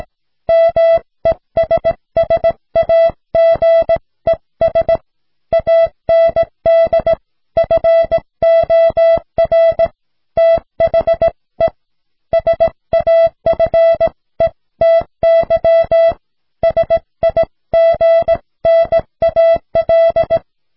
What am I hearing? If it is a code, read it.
Morse code, heard as a keyed tone, about 17 wpm, 660 Hz. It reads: MESSAGES AND FOR THE SAFETY SIGNAL